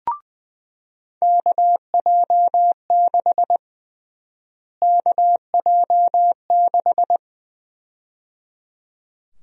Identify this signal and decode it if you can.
Morse code, heard as a keyed tone, about 20 wpm, 700 Hz. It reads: KJ6 KJ6
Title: USA Call Sign Prefixes - Repeated 2x - 20wpm